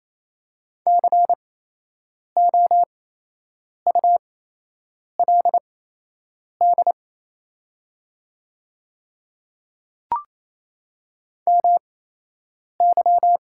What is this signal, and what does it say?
Morse code, heard as a keyed tone, about 28 wpm, spaced as for 8 wpm, 700 Hz. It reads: COULD MY
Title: 8x Character Spacing - Top 100 Words - 28wpm